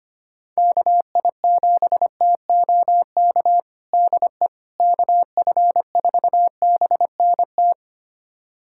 Morse code, heard as a keyed tone, about 25 wpm, 700 Hz. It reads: KI7TOK DE KF4BNT
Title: USA Callsign DE Callsign - 6 Characters - 25wpm